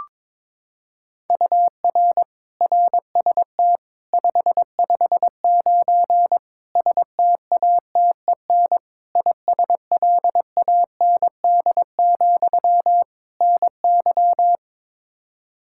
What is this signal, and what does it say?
Morse code, heard as a keyed tone, about 22 wpm, 700 Hz. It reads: UR RST 559 STATEN ISLAND, NY
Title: Practice QSO - Exchange 1 - Beginning - 22wpm